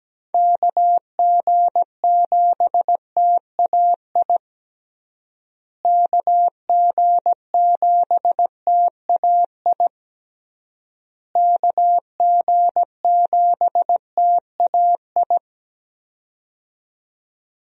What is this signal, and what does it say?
Morse code, heard as a keyed tone, about 17 wpm, 700 Hz. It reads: KG7TAI KG7TAI KG7TAI
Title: USA Call Signs - 6 Characters - Repeated 3x - 17wpm